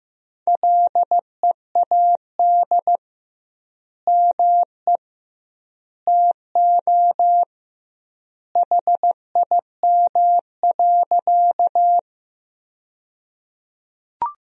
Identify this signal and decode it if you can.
Morse code, heard as a keyed tone, about 15 wpm, 700 Hz. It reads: LEAD ME TO HIM.
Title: 2x Word-Spacing - Sentences from Top 500 Words - 15wpm